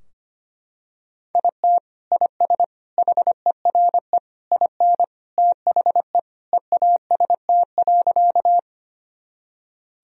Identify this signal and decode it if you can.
Morse code, heard as a keyed tone, about 25 wpm, 700 Hz. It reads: IT IS HERE IN THE EAST.
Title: Sentences from Top 1000 Words - Review Part 3 - 25wpm